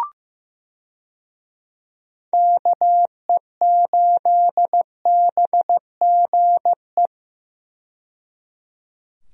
Morse code, heard as a keyed tone, about 15 wpm, 700 Hz. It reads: KE8BGE
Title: USA Call Signs - 6 Characters - 15wpm